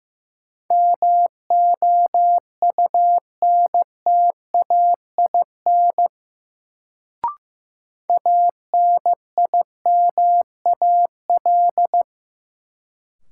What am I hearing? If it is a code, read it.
Morse code, heard as a keyed tone, about 15 wpm, 700 Hz. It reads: MOUNTAIN ANIMAL
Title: Sentences from Top 1000 Words - Review Part 1 - New Word Drill - 15wpm